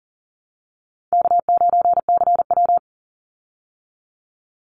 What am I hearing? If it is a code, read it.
Morse code, heard as a keyed tone, about 40 wpm, 700 Hz. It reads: K9CW
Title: Easy Call Signs Repeated 3x - 40wpm